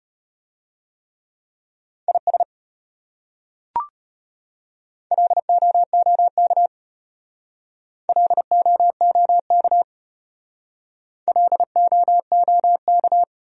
Morse code, heard as a keyed tone, about 33 wpm, 700 Hz. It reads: IS LOOK LOOK LOOK